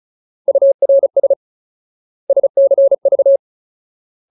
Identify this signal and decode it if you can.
Morse code, heard as a keyed tone, about 34 wpm, 550 Hz. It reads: URS SCV
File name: Ultimate Word List / 35WPM Ultimate